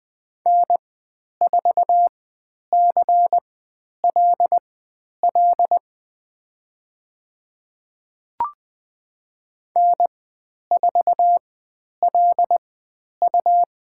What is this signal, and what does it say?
Morse code, heard as a keyed tone, about 20 wpm, spaced as for 10 wpm, 700 Hz. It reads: N4CLL N4LU
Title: 109 Lesson 18g - Letter m - Call Signs 20wpm